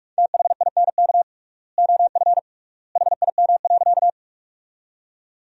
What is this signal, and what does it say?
Morse code, heard as a keyed tone, about 46 wpm, 700 Hz. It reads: THINK OF HIM.
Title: 3x Word-Spacing - Sentences from Top 200 Words - 45wpm